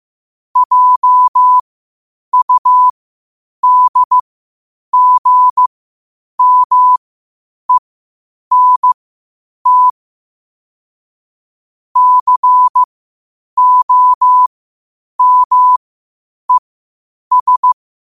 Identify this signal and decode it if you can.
Morse code, heard as a keyed tone, about 15 wpm, spaced as for 8 wpm, 1000 Hz. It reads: JUDGMENT COMES
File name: qotdfiles / 8wpm